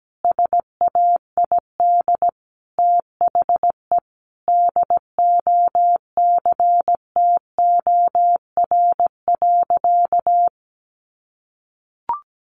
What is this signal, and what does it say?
Morse code, heard as a keyed tone, about 17 wpm, 700 Hz. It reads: SAID THE DOCTOR.
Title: Sentences from Top 1500 Words 17wpm